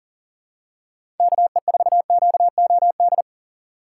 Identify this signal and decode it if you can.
Morse code, heard as a keyed tone, about 40 wpm, 700 Hz. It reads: KE4QOD
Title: USA Call Signs - 6 Characters - 3x Character Spacing - 40wpm